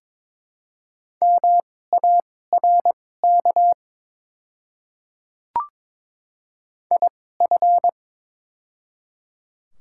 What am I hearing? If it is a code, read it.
Morse code, heard as a keyed tone, about 22 wpm, spaced as for 16 wpm, 700 Hz. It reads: MARK IF
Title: 2x Character Spacing - Top 300 Words - 22wpm